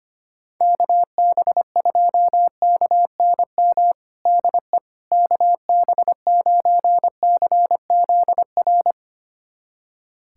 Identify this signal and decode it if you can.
Morse code, heard as a keyed tone, about 25 wpm, 700 Hz. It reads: KB2KNM DE KB9CZR
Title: USA Callsign DE Callsign - 6 Characters - 25wpm